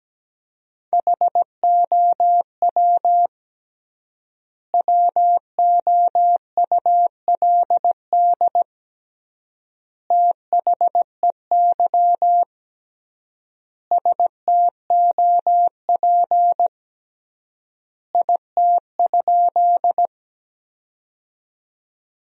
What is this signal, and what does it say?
Morse code, heard as a keyed tone, about 17 wpm, 700 Hz. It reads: HOW WOULD THEY STOP IT?
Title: 3x Word-Spacing - Sentences from Top 300 Words - 17wpm